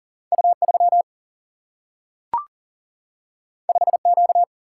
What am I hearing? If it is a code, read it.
Morse code, heard as a keyed tone, about 41 wpm, 700 Hz. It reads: U3 5Q
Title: Single-Letter Single-Number 40wpm